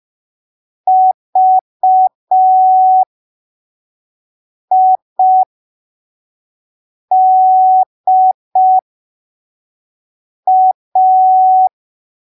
Morse code, heard as a keyed tone, about 5 wpm, spaced as for 3 wpm, 750 Hz. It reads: VIDA